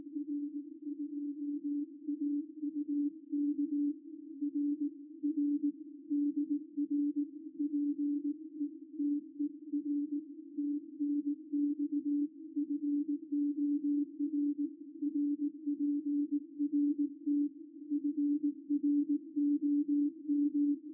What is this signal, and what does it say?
Morse code, heard as a keyed tone, about 18 wpm, 290 Hz. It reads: R2AUK RR DR PETER TNX FOR RPRT FROM
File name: r2auk-g3xjs-qrp-qso-recorded-by-ruud-vos